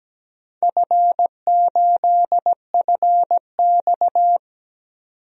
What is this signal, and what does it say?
Morse code, heard as a keyed tone, about 17 wpm, 700 Hz. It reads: F8FX
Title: DXCC-100 Call Signs - 4 Characters - 17wpm